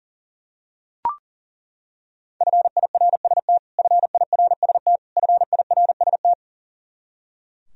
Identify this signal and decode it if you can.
Morse code, heard as a keyed tone, about 40 wpm, 700 Hz. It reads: FIRST FIRST FIRST
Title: Top 100 Words Repeated 40wpm